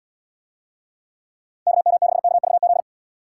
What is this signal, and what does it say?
Morse code, heard as a keyed tone, about 70 wpm, 700 Hz. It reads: KM6KVZ